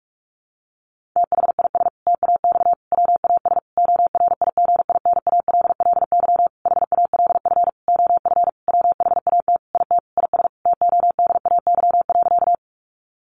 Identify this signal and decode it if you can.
Morse code, heard as a keyed tone, about 44 wpm, 700 Hz. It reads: THIS TAX WAS ORIGINALLY HALF OF WHAT IT IS TODAY.
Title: Sentences from Top 1500 Words 45wpm